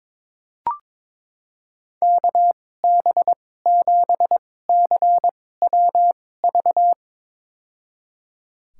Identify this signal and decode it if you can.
Morse code, heard as a keyed tone, about 22 wpm, spaced as for 16 wpm, 700 Hz. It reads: KB7CWV
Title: USA Call Signs - 6 Characters - 2x Character Spacing - 22wpm